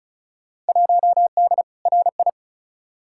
Morse code, heard as a keyed tone, about 34 wpm, 700 Hz. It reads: 1D RI
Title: Field Day Exchanges - 35wpm